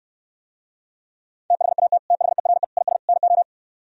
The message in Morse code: THAT THERE IS NO

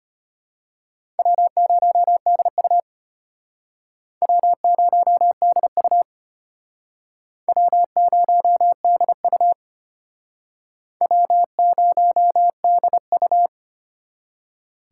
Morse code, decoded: W0DU W0DU W0DU W0DU

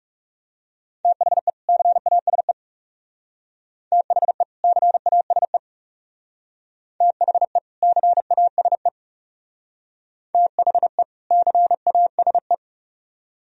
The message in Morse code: THE CASE THE CASE THE CASE THE CASE

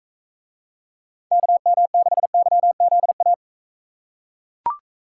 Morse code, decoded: KM6YZA

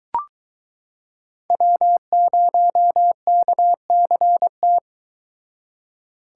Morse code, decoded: W0KCT

